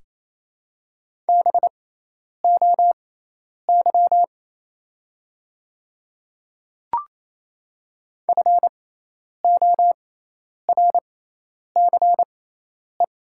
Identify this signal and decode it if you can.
Morse code, heard as a keyed tone, about 28 wpm, spaced as for 10 wpm, 700 Hz. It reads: BOY FORCE
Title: 6x Character Spacing - Top 500 Words - 28wpm